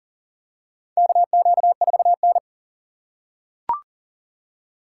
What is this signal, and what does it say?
Morse code, heard as a keyed tone, about 40 wpm, 700 Hz. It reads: KQ4N
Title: USA Call Signs - 4 Characters - 3x Character Spacing - 40wpm